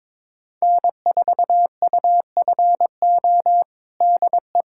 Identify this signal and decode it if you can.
Morse code, heard as a keyed tone, about 22 wpm, 700 Hz. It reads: N4UFO DE
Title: USA Callsign DE Callsign - 5 Characters - 22wpm